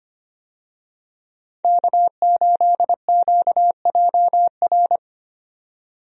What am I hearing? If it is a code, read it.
Morse code, heard as a keyed tone, about 25 wpm, 700 Hz. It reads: K8QJR
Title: USA Call Signs - 5 Characters - 25wpm